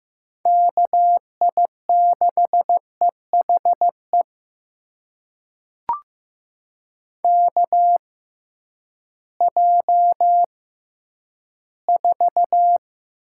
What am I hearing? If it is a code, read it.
Morse code, heard as a keyed tone, about 15 wpm, spaced as for 10 wpm, 700 Hz. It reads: KI6EHE KJ4
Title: USA Call Signs - 6 Characters - 6x Character Spacing - 15wpm